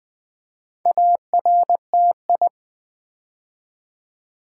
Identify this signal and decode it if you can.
Morse code, heard as a keyed tone, about 20 wpm, 700 Hz. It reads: ARTI
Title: Build A Word - 4 Letters At A Time - 10-Letter Words - 20wpm